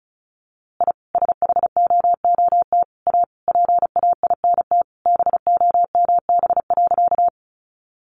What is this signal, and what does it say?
Morse code, heard as a keyed tone, about 34 wpm, 700 Hz. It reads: I SHOOT A PAINT BOMB.
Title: A Sound Of Thunder 35wpm